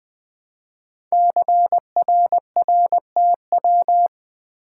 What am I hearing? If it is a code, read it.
Morse code, heard as a keyed tone, about 20 wpm, 700 Hz. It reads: CRRTW
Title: Code Groups - 5 Letters - 20wpm